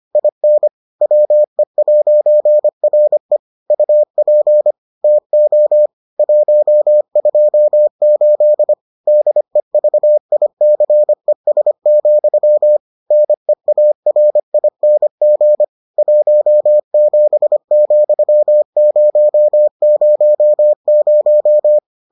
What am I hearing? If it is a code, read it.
Morse code, heard as a keyed tone, about 25 wpm, 600 Hz. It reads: IN WE'RE UP TO 128 DEVICES, NEARING 17,000